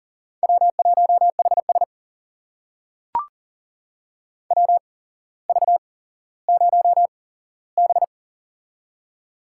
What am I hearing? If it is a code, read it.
Morse code, heard as a keyed tone, about 39 wpm, spaced as for 17 wpm, 700 Hz. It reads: W1HS WV0B
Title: USA Call Signs - 4 Characters - 8x Character Spacing - 40wpm